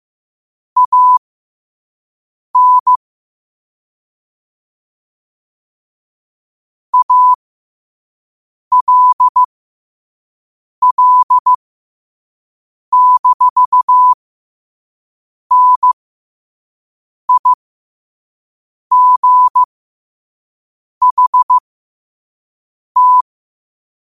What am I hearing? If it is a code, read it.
Morse code, heard as a keyed tone, about 15 wpm, spaced as for 5 wpm, 1000 Hz. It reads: AN ALL-NIGHT
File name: qotdfiles / 5wpm